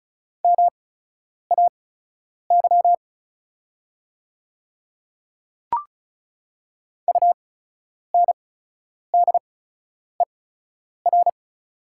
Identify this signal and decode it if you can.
Morse code, heard as a keyed tone, about 36 wpm, spaced as for 10 wpm, 700 Hz. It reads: MAY UNDER
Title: 8x Character Spacing - Top 500 Words - 35wpm